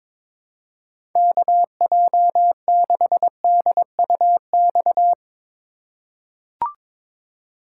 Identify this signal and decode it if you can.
Morse code, heard as a keyed tone, about 22 wpm, 700 Hz. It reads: KJ6DUX
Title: USA Call Signs - 6 Characters - 4x Character Spacing - 22wpm